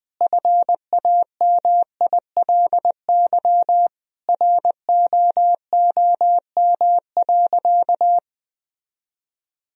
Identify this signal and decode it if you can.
Morse code, heard as a keyed tone, about 20 wpm, 700 Hz. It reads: FAMILY ROOM.
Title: Sentences from Top 1000 Words - Review Part 3 - 20wpm